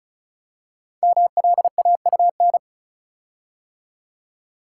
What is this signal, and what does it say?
Morse code, heard as a keyed tone, about 36 wpm, 700 Hz. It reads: MLAUN
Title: Code Groups - 5 Letters - 35wpm